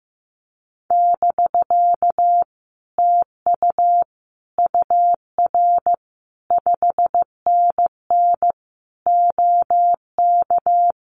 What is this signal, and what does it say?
Morse code, heard as a keyed tone, about 15 wpm, 700 Hz. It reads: <BK> TU UR 5NN OK
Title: Common QSO Phrases Encore 15wpm